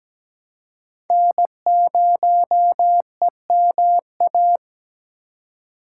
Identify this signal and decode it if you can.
Morse code, heard as a keyed tone, about 17 wpm, 700 Hz. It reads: N0EMA
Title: USA Call Signs - 5 Characters - 3x Character Spacing - 17wpm